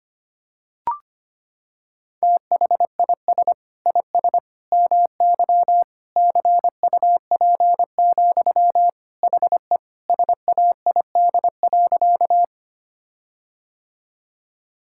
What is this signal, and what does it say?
Morse code, heard as a keyed tone, about 25 wpm, 700 Hz. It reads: THIS IS MY CUP, HE SAID.
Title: Sentences from Top 1600 Words 25wpm